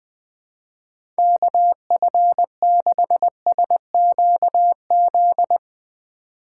Morse code, heard as a keyed tone, about 21 wpm, 700 Hz. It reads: KF6SQZ
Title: USA Call Signs - 6 Characters - 3x Character Spacing - 20wpm